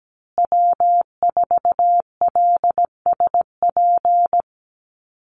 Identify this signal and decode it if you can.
Morse code, heard as a keyed tone, about 17 wpm, 700 Hz. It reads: W4LSP
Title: USA Call Signs Repeated 2x 17wpm